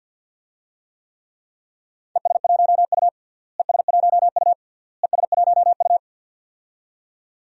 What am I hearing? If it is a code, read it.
Morse code, heard as a keyed tone, about 51 wpm, 700 Hz. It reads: ES1U ES1U ES1U
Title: DXCC-100 Call Signs - 4 Characters - Repeated 3x - 50wpm